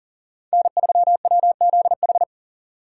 Morse code, heard as a keyed tone, about 51 wpm, 700 Hz. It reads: N3WZH